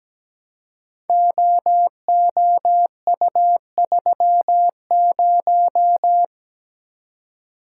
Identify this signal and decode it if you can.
Morse code, heard as a keyed tone, about 17 wpm, 700 Hz. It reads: OOU30
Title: Code Groups - 5 Letter:Numbers - 17wpm